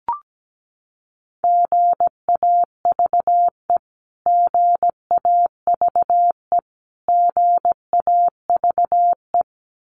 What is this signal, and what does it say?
Morse code, heard as a keyed tone, about 17 wpm, 700 Hz. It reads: GAVE GAVE GAVE
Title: Top 500 Words Repeated 17wpm